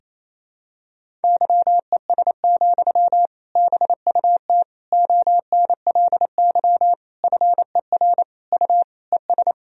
Morse code, heard as a keyed tone, about 28 wpm, 700 Hz. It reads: YES, BUT ONLY FER U ES
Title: Sentences from Top 500 Words and Abbreviations 28wpm